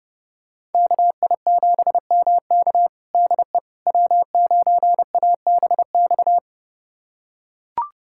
Morse code, heard as a keyed tone, about 32 wpm, 700 Hz. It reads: KI7MK DE W9ABX
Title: USA Callsign DE Callsign - 5 Characters - 30wpm